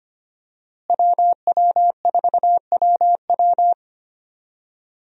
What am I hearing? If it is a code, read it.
Morse code, heard as a keyed tone, about 25 wpm, 700 Hz. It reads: WW4WW
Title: USA Call Signs - 5 Characters - Repeated 2x - 25wpm